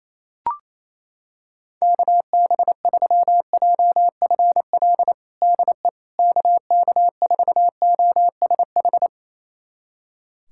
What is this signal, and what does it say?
Morse code, heard as a keyed tone, about 27 wpm, 700 Hz. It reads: KB3JFL DE KK4OSH
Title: USA Callsign DE Callsign 28wpm